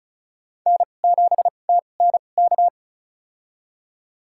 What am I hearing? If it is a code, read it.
Morse code, heard as a keyed tone, about 36 wpm, spaced as for 25 wpm, 700 Hz. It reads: N7TNK
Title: USA Call Signs - 5 Characters - 2x Character Spacing - 35wpm